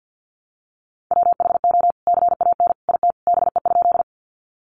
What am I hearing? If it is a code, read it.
Morse code, heard as a keyed tone, about 50 wpm, 700 Hz. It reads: WHO CAN IT BE?
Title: Sentences from Top 100 Words 50wpm